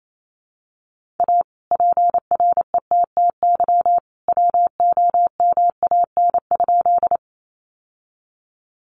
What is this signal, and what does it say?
Morse code, heard as a keyed tone, about 28 wpm, 700 Hz. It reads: A PRETTY WOMAN?
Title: Sentences from Top 1100 Words 28wpm